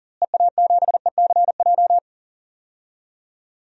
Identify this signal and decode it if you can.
Morse code, heard as a keyed tone, about 40 wpm, 700 Hz. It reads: EA7ECJ
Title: DXCC-100 Call Signs - 6 Characters - Repeated 3x - 40wpm